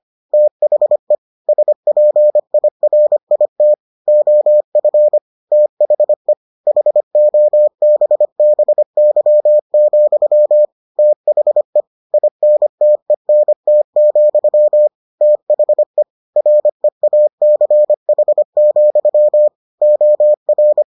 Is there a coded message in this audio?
Yes: Morse code from 0.3 to 20.8 s, heard as a keyed tone, about 25 wpm, 600 Hz. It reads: THE SPIRIT OF THE HOBBY, THE INTENT, THE REACH, OR